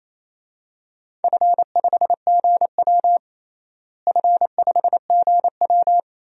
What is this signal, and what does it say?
Morse code, heard as a keyed tone, about 28 wpm, 700 Hz. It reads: F5GW F5GW
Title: DXCC-100 Call Signs - 4 Characters - Repeated 2x - 28wpm